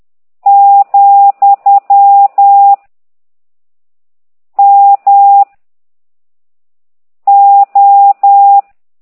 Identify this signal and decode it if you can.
Morse code, heard as a keyed tone, about 10 wpm, 800 Hz. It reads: , M O
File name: ct2idw / exerc13